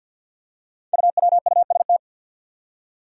Morse code, heard as a keyed tone, about 52 wpm, 700 Hz. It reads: UWUST
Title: Five Challenging Letter-Number Combinations 50wpm